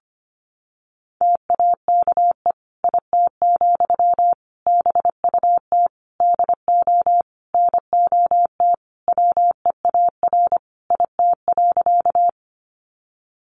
Morse code, heard as a keyed tone, about 25 wpm, 700 Hz. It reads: TAKE IT, BUT DO NOT WEAR IT.
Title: Sentences from Top 1500 Words 25wpm